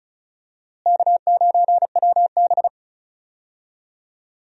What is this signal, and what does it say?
Morse code, heard as a keyed tone, about 34 wpm, 700 Hz. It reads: K9WB
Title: USA Call Signs - 4 Characters - Repeated 2x - 35wpm